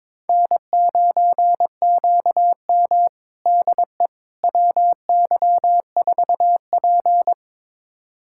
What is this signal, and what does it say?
Morse code, heard as a keyed tone, about 22 wpm, 700 Hz. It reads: N9QM DE WY4P
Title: USA Callsign DE Callsign - 4 Characters - 22wpm